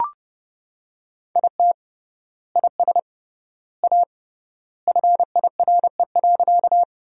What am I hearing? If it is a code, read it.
Morse code, heard as a keyed tone, about 29 wpm, 700 Hz. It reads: IT IS A FIRE.
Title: 3x Word-Spacing - Sentences from Top 400 Words - 30wpm